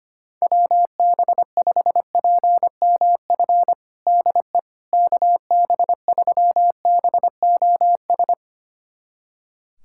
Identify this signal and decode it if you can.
Morse code, heard as a keyed tone, about 25 wpm, 700 Hz. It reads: WB5PMF DE KB3BOS
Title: USA Callsign DE Callsign 25wpm